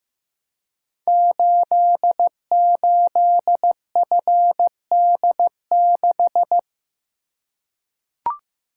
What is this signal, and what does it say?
Morse code, heard as a keyed tone, about 15 wpm, 700 Hz. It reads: 88FD6
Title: Code Groups - 5 Letter:Numbers - 15wpm